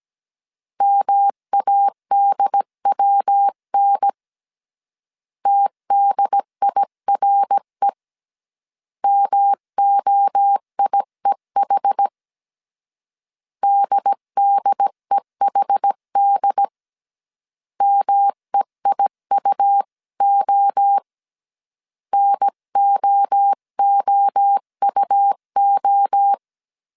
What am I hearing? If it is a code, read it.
Morse code, heard as a keyed tone, about 17 wpm, 800 Hz. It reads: MADWN TDIRE MOIEH DDEHD MEIUO NOOUO